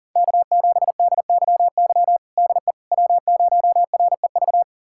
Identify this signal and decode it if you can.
Morse code, heard as a keyed tone, about 40 wpm, 700 Hz. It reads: K7DYY DE W0REV
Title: USA Callsign DE Callsign - 5 Characters - 40wpm